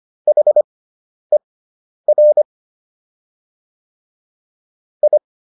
Morse code, heard as a keyed tone, about 25 wpm, spaced as for 10 wpm, 600 Hz. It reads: HER I